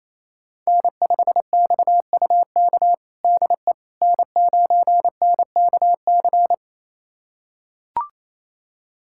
Morse code, heard as a keyed tone, about 28 wpm, 700 Hz. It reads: N5XUK DE N9NKC
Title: USA Callsign DE Callsign - 5 Characters - 28wpm